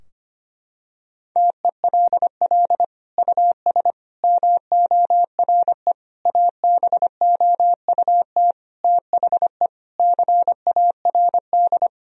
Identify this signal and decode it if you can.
Morse code, heard as a keyed tone, about 25 wpm, 700 Hz. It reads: TELL US MORE ABOUT THE CARD
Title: Sentences from Top 1000 Words - Review Part 4 - 25wpm